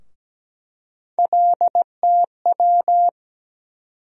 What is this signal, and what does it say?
Morse code, heard as a keyed tone, about 17 wpm, 700 Hz. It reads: LTW
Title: USA Call Sign Postfixes - 3 Letters 17wpm